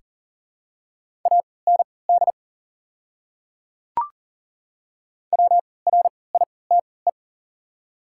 Morse code, heard as a keyed tone, about 37 wpm, spaced as for 22 wpm, 690 Hz. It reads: AND WRITE
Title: 3x Character Spacing - Top 100 Words - 40wpm